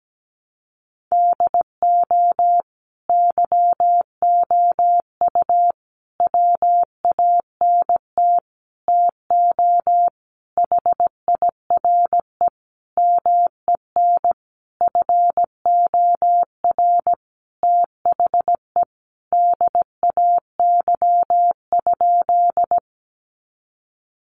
Morse code, heard as a keyed tone, about 17 wpm, 700 Hz. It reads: DO YOU WANT TO HIRE MEN FOR THE DAY?
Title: Sentences from Top 1400 Words 17wpm